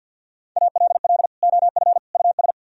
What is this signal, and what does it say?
Morse code, heard as a keyed tone, about 62 wpm, 700 Hz. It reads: ALL OF US
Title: Speed-Racing - Sets of 3 Words 50wpm